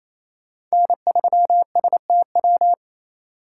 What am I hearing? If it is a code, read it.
Morse code, heard as a keyed tone, about 28 wpm, 700 Hz. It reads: N3STW
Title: USA Call Signs - 5 Characters - 2x Character Spacing - 28wpm